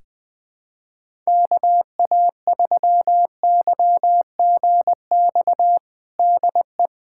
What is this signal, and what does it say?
Morse code, heard as a keyed tone, about 20 wpm, 700 Hz. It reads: KA3YGX DE
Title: USA Callsign DE Callsign 20wpm